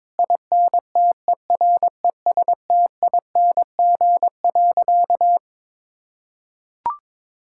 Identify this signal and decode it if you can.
Morse code, heard as a keyed tone, about 23 wpm, 700 Hz. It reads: INTERESTING.
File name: practice / 22wpm / Sentences with 2-Word Phrases - 22wpm